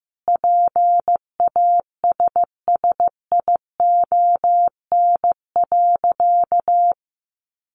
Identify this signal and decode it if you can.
Morse code, heard as a keyed tone, about 15 wpm, 700 Hz. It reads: PASSION.
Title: Sentences from Top 1300 Words 15wpm